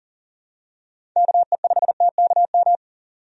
Morse code, heard as a keyed tone, about 40 wpm, 700 Hz. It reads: KE5TKM
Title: USA Call Signs - 6 Characters - 40wpm